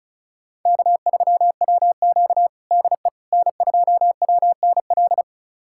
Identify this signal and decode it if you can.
Morse code, heard as a keyed tone, about 36 wpm, 700 Hz. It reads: K3WQ DE N2WNL
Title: USA Callsign DE Callsign 35wpm